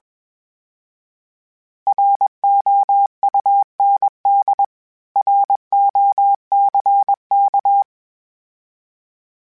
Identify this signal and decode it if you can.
Morse code, heard as a keyed tone, about 21 wpm, 800 Hz. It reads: ROUND ROCK
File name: fromSMs / FromSMs mp3s / Cities@20